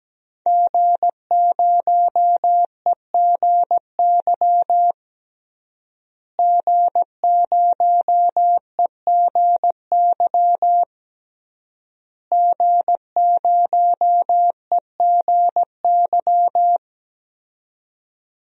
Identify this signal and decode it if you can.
Morse code, heard as a keyed tone, about 17 wpm, 700 Hz. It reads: G0EGY G0EGY G0EGY
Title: DXCC-100 Call Signs - 5 Characters - Repeated 3x - 17wpm